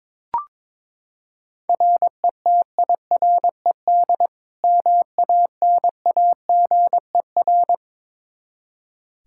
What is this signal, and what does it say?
Morse code, heard as a keyed tone, about 22 wpm, 700 Hz. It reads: RETIRED MANAGER
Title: K1USN SST QSO Practice 22wpm